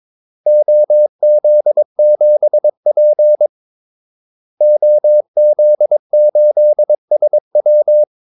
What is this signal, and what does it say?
Morse code, heard as a keyed tone, about 22 wpm, 600 Hz. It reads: OZ7P OZ8SW
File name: Callsigns-22wpm